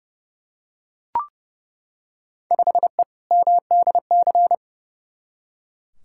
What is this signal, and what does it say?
Morse code, heard as a keyed tone, about 30 wpm, 700 Hz. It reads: 5E MDC
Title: Field Day Exchanges - 30wpm